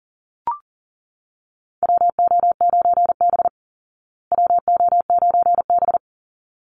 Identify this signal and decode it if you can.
Morse code, heard as a keyed tone, about 40 wpm, 700 Hz. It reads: WO9B WO9B
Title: Call Signs Repeated 3x - 40wpm